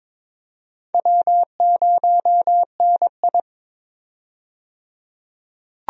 Morse code, heard as a keyed tone, about 22 wpm, 700 Hz. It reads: W0NI
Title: USA Call Signs - 4 Characters - 22wpm